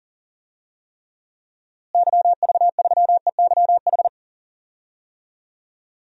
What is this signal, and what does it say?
Morse code, heard as a keyed tone, about 40 wpm, 700 Hz. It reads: YV3EYH
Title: DXCC-100 Call Signs - 6 Characters - Repeated 2x - 40wpm